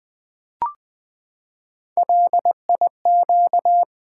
Morse code, heard as a keyed tone, about 20 wpm, 700 Hz. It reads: LIQ
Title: Build A Word - 3 Letters At A Time - 6-Letter Words - 20wpm